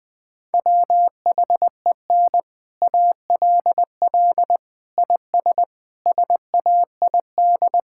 Morse code, heard as a keyed tone, about 20 wpm, 700 Hz. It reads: WHEN ALL IS SAID
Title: Sets of 6 Words - 20wpm